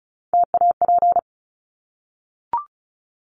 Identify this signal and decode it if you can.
Morse code, heard as a keyed tone, about 34 wpm, 700 Hz. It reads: TAP
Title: Top 1251-1500 Words 35wpm